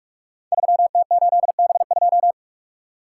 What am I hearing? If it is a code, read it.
Morse code, heard as a keyed tone, about 44 wpm, 700 Hz. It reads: 3T8B2
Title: Code Groups - 5 Letter:Numbers - 45wpm